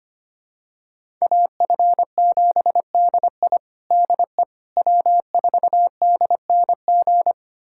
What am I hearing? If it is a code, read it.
Morse code, heard as a keyed tone, about 25 wpm, 700 Hz. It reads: AF7DI DE W4DNG
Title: USA Callsign DE Callsign - 5 Characters - 25wpm